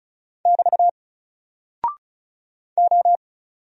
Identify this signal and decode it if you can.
Morse code, heard as a keyed tone, about 36 wpm, 700 Hz. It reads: <BT> O
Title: Single Character (Letter:Number:Punctuation:Prosign) 35wpm